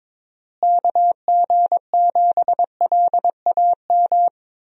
Morse code, heard as a keyed tone, about 22 wpm, 700 Hz. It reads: KG7LAM
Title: USA Call Signs - 6 Characters - Repeated 3x - 22wpm